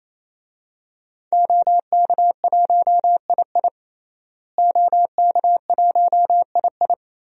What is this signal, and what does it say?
Morse code, heard as a keyed tone, about 27 wpm, 700 Hz. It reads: OK1II OK1II
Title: DXCC-100 Call Signs - 5 Characters - Repeated 3x - 28wpm